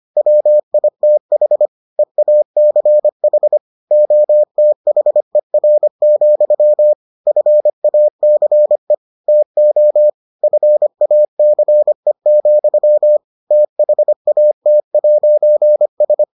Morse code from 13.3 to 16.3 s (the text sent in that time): THAT'S